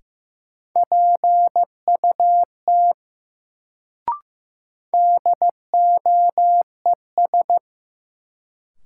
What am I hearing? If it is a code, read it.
Morse code, heard as a keyed tone, about 15 wpm, 700 Hz. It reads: PUT DOES
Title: Sentences from Top 200 Words - New Word Drill - 15wpm